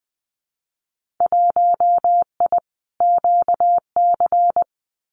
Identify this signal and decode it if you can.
Morse code, heard as a keyed tone, about 20 wpm, 700 Hz. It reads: 1I QC
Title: Winter Field Day Exchanges 20wpm